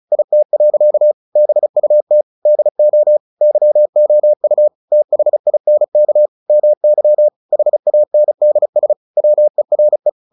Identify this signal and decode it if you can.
Morse code, heard as a keyed tone, about 34 wpm, 600 Hz. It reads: IT. BUT DO YOU THINK MY HANDS WERE
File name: the old man and the sea 2 35wpm